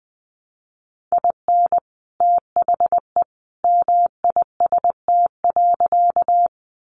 Morse code, heard as a keyed tone, about 20 wpm, 700 Hz. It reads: IN THE MIST.
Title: A Sound Of Thunder 20wpm